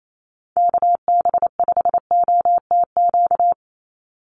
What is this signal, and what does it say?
Morse code, heard as a keyed tone, about 28 wpm, 700 Hz. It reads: KB5OTQ
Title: USA Call Signs 28wpm